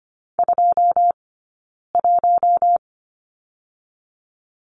Morse code, heard as a keyed tone, about 25 wpm, 700 Hz. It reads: 2 1
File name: Lesson 15d - Number 5 - Two Characters 25wpm